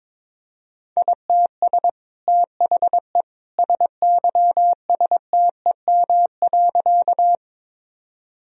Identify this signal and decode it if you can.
Morse code, heard as a keyed tone, about 22 wpm, 700 Hz. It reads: ITS THE SYSTEM.
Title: Sentences from Top 1000 Words - Review Part 3 - 22wpm